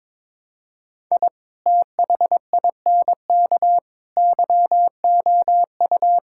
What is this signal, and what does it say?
Morse code, heard as a keyed tone, about 22 wpm, 700 Hz. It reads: I THINK YOU
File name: practice / 22wpm / Build A Phrase - 5-Word Phrases - 22wpm